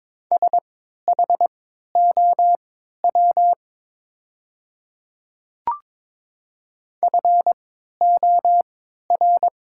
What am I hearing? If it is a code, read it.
Morse code, heard as a keyed tone, about 22 wpm, spaced as for 13 wpm, 700 Hz. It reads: SHOW FOR
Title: 3x Character Spacing - Top 200 Words - 22wpm